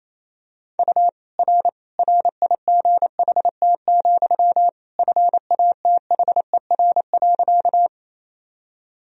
Morse code, heard as a keyed tone, about 27 wpm, 700 Hz. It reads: U R RIGHT, FATHER.
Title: Sentences from Top 300 Words and Abbreviations 28wpm